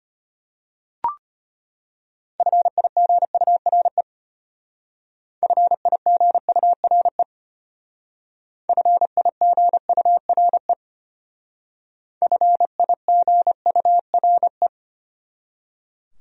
Morse code, heard as a keyed tone, about 32 wpm, 700 Hz. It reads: FIGURE FIGURE FIGURE FIGURE